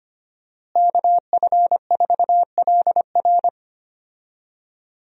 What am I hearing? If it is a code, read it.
Morse code, heard as a keyed tone, about 25 wpm, 700 Hz. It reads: KF4LR
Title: USA Call Signs - 5 Characters - Repeated 3x - 25wpm